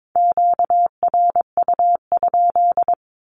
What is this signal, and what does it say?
Morse code, heard as a keyed tone, about 22 wpm, 700 Hz. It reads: QRU?
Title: Q-Codes with Spoken Meaning - 22wpm